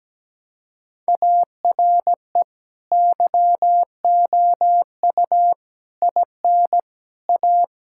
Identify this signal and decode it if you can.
Morse code, heard as a keyed tone, about 17 wpm, 700 Hz. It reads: ARE YOU IN A
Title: Sentences from Top 1000 Words - Review Part 5 - 17wpm